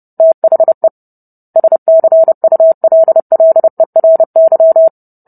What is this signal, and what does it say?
Morse code, heard as a keyed tone, about 30 wpm, 650 Hz. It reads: THE SCULLERY